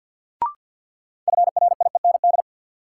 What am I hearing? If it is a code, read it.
Morse code, heard as a keyed tone, about 53 wpm, 700 Hz. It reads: FRIEND
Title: Sentences from Top 300 Words - New Word Drill - 50wpm